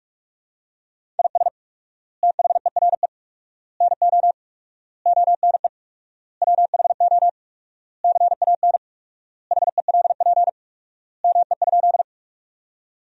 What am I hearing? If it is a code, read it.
Morse code, heard as a keyed tone, about 46 wpm, 700 Hz. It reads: IS THERE NO ONE WHO CAN HELP ME?
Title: 4x Word-Spacing - Sentences from Top 200 Words - 45wpm